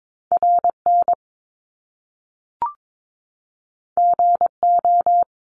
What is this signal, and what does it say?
Morse code, heard as a keyed tone, about 22 wpm, 700 Hz. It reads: RN GO
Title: Top Two Letter Combinations in Words 22wpm